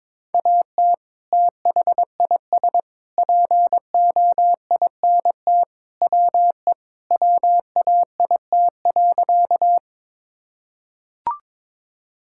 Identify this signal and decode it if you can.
Morse code, heard as a keyed tone, about 22 wpm, 700 Hz. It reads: AT THIS POINT WE WAIT.
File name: practice / 22wpm / Sentences with 3-Word Phrases - 22wpm